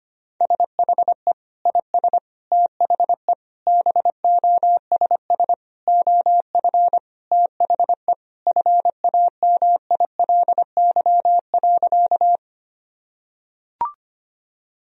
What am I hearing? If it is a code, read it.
Morse code, heard as a keyed tone, about 25 wpm, 700 Hz. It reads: SHE IS THE BOSS OF THE FAMILY.
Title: Sentences from Top 1000 Words - Review Part 4 - 25wpm